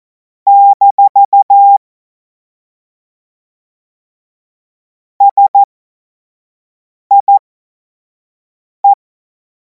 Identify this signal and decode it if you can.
Morse code, heard as a keyed tone, about 14 wpm, spaced as for 5 wpm, 800 Hz. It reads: - SIE
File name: Morsekurs / Witze / Witz 31. März 2026 25 F70